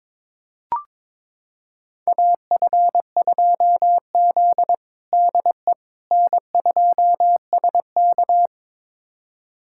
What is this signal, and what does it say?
Morse code, heard as a keyed tone, about 22 wpm, 700 Hz. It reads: AF2Z DE N2SK
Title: USA Callsign DE Callsign - 4 Characters - 22wpm